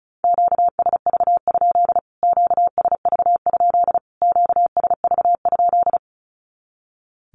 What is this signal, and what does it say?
Morse code, heard as a keyed tone, about 34 wpm, 700 Hz. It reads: QSV? QSV? QSV?
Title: Q-Codes Repeated 3x 35wpm